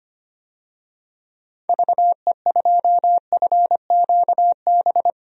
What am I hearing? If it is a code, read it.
Morse code, heard as a keyed tone, about 25 wpm, 700 Hz. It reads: VE2FQB
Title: DXCC-100 Call Signs - 6 Characters - Repeated 2x - 25wpm